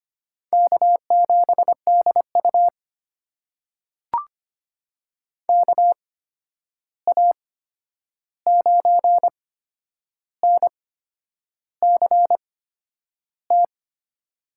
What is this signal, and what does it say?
Morse code, heard as a keyed tone, about 25 wpm, spaced as for 9 wpm, 700 Hz. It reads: K7DU KA9NCT
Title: USA Call Signs - 8x Character Spacing - 25wpm